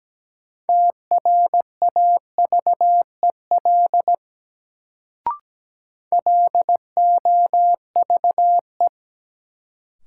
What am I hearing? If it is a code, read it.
Morse code, heard as a keyed tone, about 17 wpm, 700 Hz. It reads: TRAVEL LOVE
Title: Sentences from Top 400 Words - New Word Drill - 17wpm